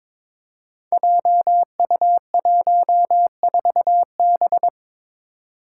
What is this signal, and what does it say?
Morse code, heard as a keyed tone, about 22 wpm, 700 Hz. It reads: JU14B
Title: Five Challenging Letter-Number Combinations 22wpm